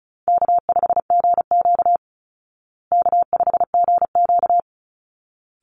Morse code, heard as a keyed tone, about 34 wpm, 700 Hz. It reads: K5GQ K5GQ
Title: Call Signs Repeated 3x - 35wpm